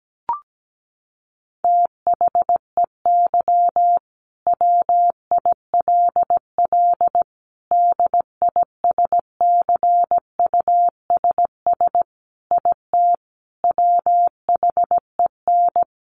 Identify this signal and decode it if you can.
Morse code, heard as a keyed tone, about 17 wpm, 700 Hz. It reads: THEY WILL DISCUSS IT WHEN
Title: Sentences from Top 1400 Words 17wpm